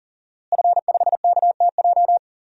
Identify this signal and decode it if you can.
Morse code, heard as a keyed tone, about 40 wpm, 700 Hz. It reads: F5KTJ
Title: DXCC-100 Call Signs - 5 Characters - Repeated 3x - 40wpm